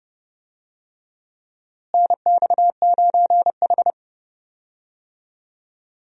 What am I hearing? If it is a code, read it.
Morse code, heard as a keyed tone, about 29 wpm, 700 Hz. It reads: NX9H
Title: USA Call Signs - 4 Characters - Repeated 3x - 30wpm